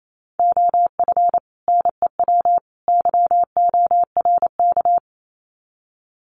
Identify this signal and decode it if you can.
Morse code, heard as a keyed tone, about 28 wpm, 700 Hz. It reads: OF NEW YORK
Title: Sets of 3 Words 28wpm